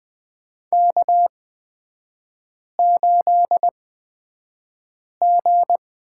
Morse code, heard as a keyed tone, about 20 wpm, spaced as for 5 wpm, 700 Hz. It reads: K8G
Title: 250 Lesson 40e - BK - Three Characters 20wpm